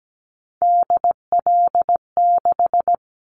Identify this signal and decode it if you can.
Morse code, heard as a keyed tone, about 17 wpm, 700 Hz. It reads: DL6
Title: Callsign Prefixes for Top 25 DXCC - 17wpm